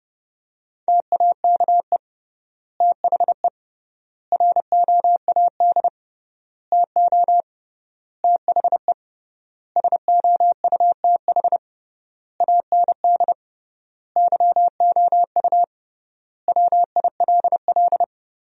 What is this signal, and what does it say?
Morse code, heard as a keyed tone, about 30 wpm, 700 Hz. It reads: TAKE THE ROAD TO THE SOUTH AND YOU WILL
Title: 3x Word-Spacing - Sentences from Top 400 Words - 30wpm